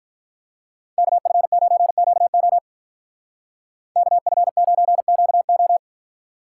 Morse code, heard as a keyed tone, about 43 wpm, 700 Hz. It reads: KF9QO KF9QO
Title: Speed Racing - USA 2x2 Call Signs 35wpm